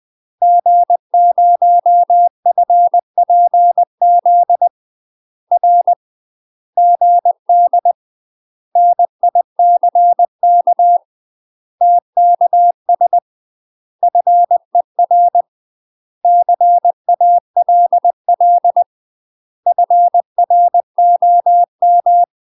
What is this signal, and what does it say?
Morse code, heard as a keyed tone, about 20 wpm, 700 Hz. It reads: G0FPZ R GD NICK TKS FER CALL FROM
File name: UK MASTER QSOs TRAINING / 20WPM / UK MASTER QSOs 0052